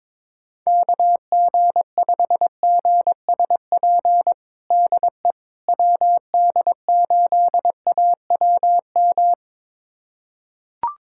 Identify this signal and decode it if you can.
Morse code, heard as a keyed tone, about 22 wpm, 700 Hz. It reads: KG5GSP DE WD8AWM
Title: USA Callsign DE Callsign 22wpm